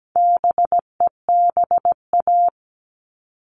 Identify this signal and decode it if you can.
Morse code, heard as a keyed tone, about 17 wpm, 700 Hz. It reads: BEBA
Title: All Cut Numbers - 4-Digit 17wpm